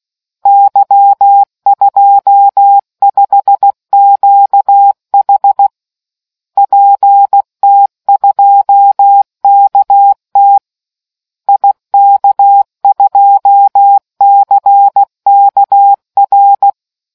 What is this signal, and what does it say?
Morse code, heard as a keyed tone, about 16 wpm, 800 Hz. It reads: Y25QH PT2KT IK2CKR